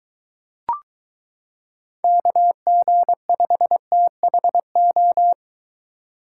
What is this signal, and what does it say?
Morse code, heard as a keyed tone, about 23 wpm, 700 Hz. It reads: KG5THO